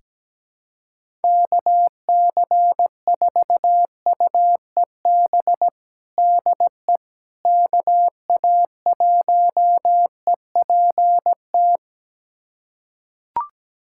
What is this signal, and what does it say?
Morse code, heard as a keyed tone, about 17 wpm, 700 Hz. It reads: KC4UEB DE KA1EPT
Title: USA Callsign DE Callsign - 6 Characters - 17wpm